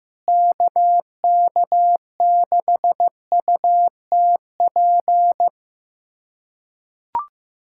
Morse code, heard as a keyed tone, about 15 wpm, 700 Hz. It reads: KK6UTP
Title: USA Callsign DE Callsign - 6 Characters - 15wpm